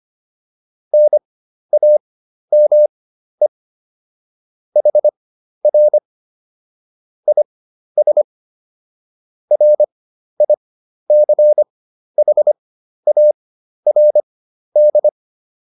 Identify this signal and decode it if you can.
Morse code, heard as a keyed tone, about 25 wpm, spaced as for 12 wpm, 600 Hz. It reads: NAME HR IS RICHARD